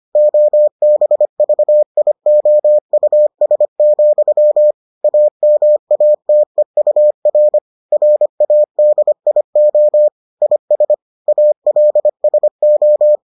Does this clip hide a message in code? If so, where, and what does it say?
Yes: Morse code from 0.2 to 13.2 s, heard as a keyed tone, about 25 wpm, 600 Hz. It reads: OBVIOUS, AMATEUR RADIO IS ALSO